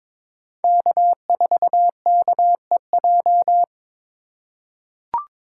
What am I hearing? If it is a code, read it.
Morse code, heard as a keyed tone, about 22 wpm, 700 Hz. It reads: K4KEJ
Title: USA Call Signs - 5 Characters - 4x Character Spacing - 22wpm